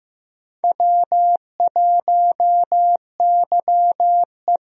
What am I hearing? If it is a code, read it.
Morse code, heard as a keyed tone, about 15 wpm, 700 Hz. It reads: W1YE
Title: USA Call Signs - 4 Characters - 15wpm